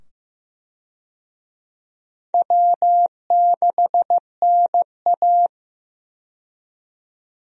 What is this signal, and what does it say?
Morse code, heard as a keyed tone, about 15 wpm, 700 Hz. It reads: W6NA
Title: USA Call Signs - 4 Characters - 15wpm